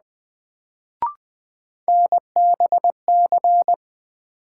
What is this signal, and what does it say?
Morse code, heard as a keyed tone, about 20 wpm, 700 Hz. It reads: NBC
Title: Sets of 5 Words Encore - New Word Drill - 20wpm